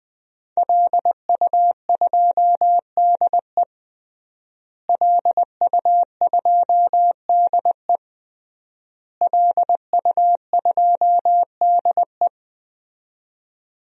Morse code, heard as a keyed tone, about 20 wpm, 700 Hz. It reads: LU2DE LU2DE LU2DE
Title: DXCC-100 Call Signs - 5 Characters - Repeated 3x - 20wpm